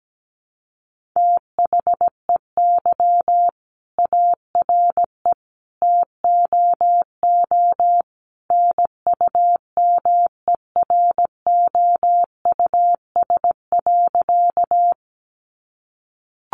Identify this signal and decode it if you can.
Morse code, heard as a keyed tone, about 17 wpm, 700 Hz. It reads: THEY ARE TOO NUMEROUS.
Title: Sentences from Top 1400 Words 17wpm